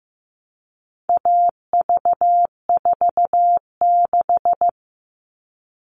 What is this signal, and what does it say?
Morse code, heard as a keyed tone, about 15 wpm, 700 Hz. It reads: AV46
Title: All Cut Numbers - 4-Digit 15wpm